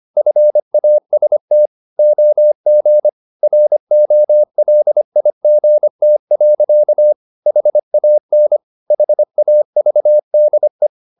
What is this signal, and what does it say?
Morse code, heard as a keyed tone, about 25 wpm, 600 Hz. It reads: FAST OG ROLIGT. HAN HAVDE
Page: Ferie 24 wpm